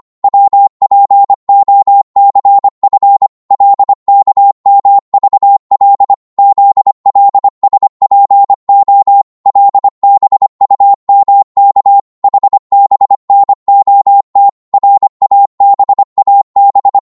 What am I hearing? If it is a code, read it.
Morse code, heard as a keyed tone, about 25 wpm, 800 Hz. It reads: WPOCF LKMVL ZLSPO LBUMK HBNOT RABAB
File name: Morse / ltr-25wpm